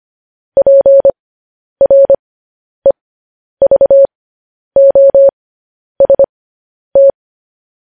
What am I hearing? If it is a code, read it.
Morse code, heard as a keyed tone, about 25 wpm, spaced as for 10 wpm, 550 Hz. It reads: PREVOST
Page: news-headlines-in-morse-code-at-10-wpm